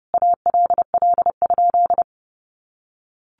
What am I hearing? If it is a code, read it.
Morse code, heard as a keyed tone, about 29 wpm, 700 Hz. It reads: ALL?
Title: Sentences from Top 800 Words 30wpm